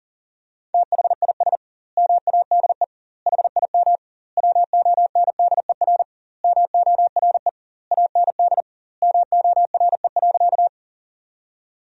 Morse code, heard as a keyed tone, about 38 wpm, 700 Hz. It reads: THIS MADE HIM WONDER MORE AND MORE.
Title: 2x Word-Spacing - Sentences from Top 500 Words - 40wpm